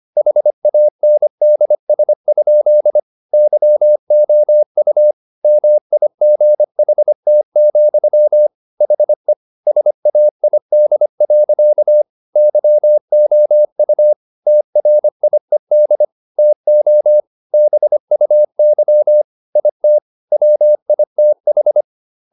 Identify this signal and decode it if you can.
Morse code, heard as a keyed tone, about 25 wpm, 600 Hz. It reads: HANDS? YOU MIGHT, HE SAID. YOU TRIED TO BUY IT WITH